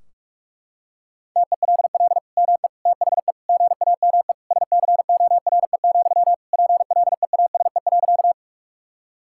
Morse code, heard as a keyed tone, about 46 wpm, 700 Hz. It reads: TELL ME THE GAME SCORE, PLEASE.
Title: Sentences from Top 1600 Words 45wpm